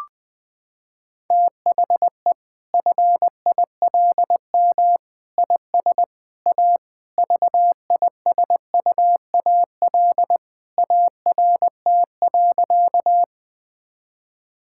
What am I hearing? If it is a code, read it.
Morse code, heard as a keyed tone, about 20 wpm, 700 Hz. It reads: THE FILM IS A VISUAL ART.
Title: Sentences from Top 1000 Words - Review Part 6 - 20wpm